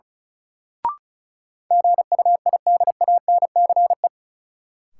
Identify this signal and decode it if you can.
Morse code, heard as a keyed tone, about 36 wpm, 700 Hz. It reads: GUIDANCE
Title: Sentences from Top 1600 Words - New Word Drill - 35wpm